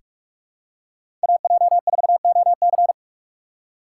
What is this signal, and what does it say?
Morse code, heard as a keyed tone, about 46 wpm, 700 Hz. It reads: AJ4OC
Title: USA Call Signs - 3x Character Spacing - 45wpm